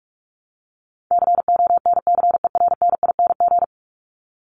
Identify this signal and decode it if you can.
Morse code, heard as a keyed tone, about 44 wpm, 700 Hz. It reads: CONCERNING
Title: ING Suffix - Top 100 Words with Base Words 45wpm